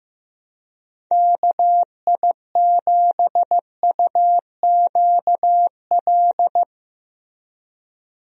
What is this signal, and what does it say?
Morse code, heard as a keyed tone, about 15 wpm, 700 Hz. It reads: KI7UQL
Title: USA Call Signs - 6 Characters - Repeated 2x - 15wpm